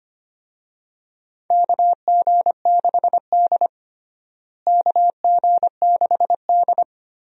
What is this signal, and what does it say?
Morse code, heard as a keyed tone, about 25 wpm, 700 Hz. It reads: KG6D KG6D
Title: USA Call Signs - 4 Characters - Repeated 2x - 25wpm